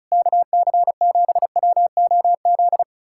Morse code, heard as a keyed tone, about 40 wpm, 700 Hz. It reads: KC7WOZ